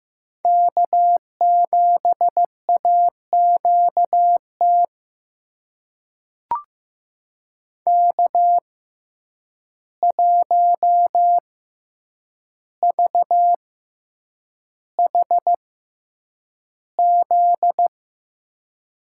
Morse code, heard as a keyed tone, about 15 wpm, spaced as for 8 wpm, 700 Hz. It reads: K7AQT K1VHZ
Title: USA Call Signs - 5 Characters - 6x Character Spacing - 15wpm